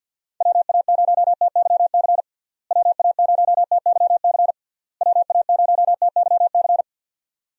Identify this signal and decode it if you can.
Morse code, heard as a keyed tone, about 45 wpm, 700 Hz. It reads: WA0TYC WA0TYC WA0TYC
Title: USA Call Signs - 6 Characters - Repeated 3x - 50wpm